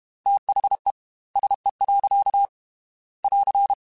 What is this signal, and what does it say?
Morse code, heard as a keyed tone, about 32 wpm, 800 Hz. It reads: THE SE. <AR>
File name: Weather Sets / Weather Sets-mp3s / Weather 4 17 @30